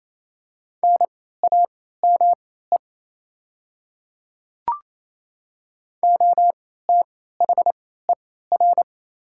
Morse code, heard as a keyed tone, about 28 wpm, spaced as for 16 wpm, 700 Hz. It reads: NAME OTHER
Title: 3x Character Spacing - Top 200 Words - 28wpm